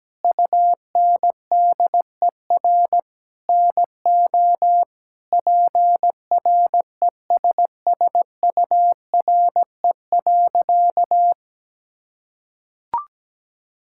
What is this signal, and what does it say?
Morse code, heard as a keyed tone, about 17 wpm, 700 Hz. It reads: UNDER NO PRESSURE.
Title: Sentences from Top 1000 Words - Review Part 6 - 17wpm